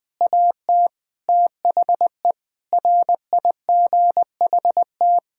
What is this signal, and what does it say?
Morse code, heard as a keyed tone, about 21 wpm, 700 Hz. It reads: AT THE RIGHT
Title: Sentences with 9-Word Phrases - 20wpm